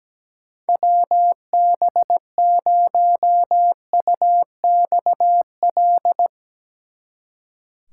Morse code, heard as a keyed tone, about 17 wpm, 700 Hz. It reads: WB0UXL
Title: Speed Racing - USA 2x3 Call Signs 17wpm